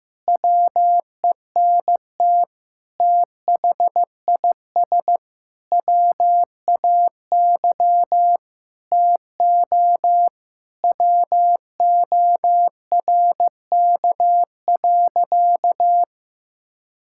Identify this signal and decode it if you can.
Morse code, heard as a keyed tone, about 15 wpm, 700 Hz. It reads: WENT THIS WAY TO WORK.
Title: Sentences from Top 1000 Words - Review Part 5 - 15wpm